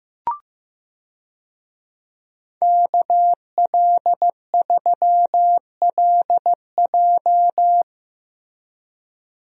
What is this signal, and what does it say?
Morse code, heard as a keyed tone, about 15 wpm, 700 Hz. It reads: KL3LJ
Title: USA Call Signs - 5 Characters - Repeated 2x - 15wpm